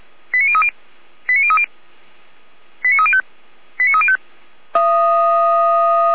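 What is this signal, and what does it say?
Five-tone selective call (ZVEI), heard as quick runs of five tones, tones of about 70 ms. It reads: 89030 89030 89305 89305
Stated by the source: ZVEI five-tone selective call received on an SDR, from the Feuerwehr Orlamünde fire brigade audio server.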